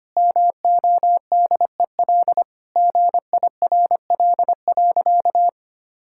Morse code, heard as a keyed tone, about 25 wpm, 700 Hz. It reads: MODEL GIRL.
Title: Sentences from Top 1000 Words - Review Part 5 - 25wpm